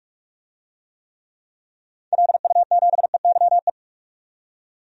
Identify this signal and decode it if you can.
Morse code, heard as a keyed tone, about 41 wpm, 700 Hz. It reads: LU7EYE
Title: DXCC-100 Call Signs - 6 Characters - 45wpm